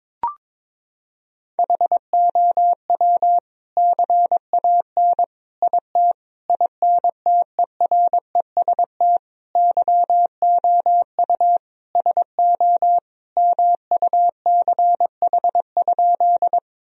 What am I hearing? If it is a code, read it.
Morse code, heard as a keyed tone, about 22 wpm, 700 Hz. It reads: HOW CAN IT INTEREST YOU SO MUCH?
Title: Sentences from Top 1000 Words - Review Part 2 - 22wpm